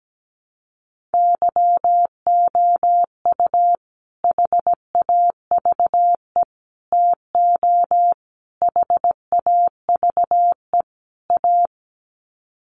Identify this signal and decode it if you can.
Morse code, heard as a keyed tone, about 17 wpm, 700 Hz. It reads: YOU HAVE TO HAVE A
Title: Sets of 5 Words Encore 17wpm